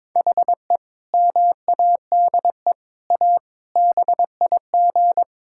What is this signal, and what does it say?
Morse code, heard as a keyed tone, about 21 wpm, 700 Hz. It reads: HE MADE A BIG
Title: Sentences from Top 1000 Words - Review Part 6 - 22wpm